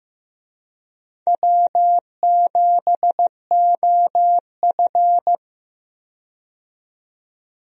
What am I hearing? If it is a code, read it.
Morse code, heard as a keyed tone, about 15 wpm, 700 Hz. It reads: W7OF
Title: USA Call Signs - 4 Characters - 15wpm